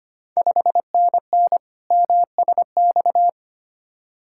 Morse code, heard as a keyed tone, about 25 wpm, 690 Hz. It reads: 5NN MSX
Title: Ontario QSO Party Exchange 25wpm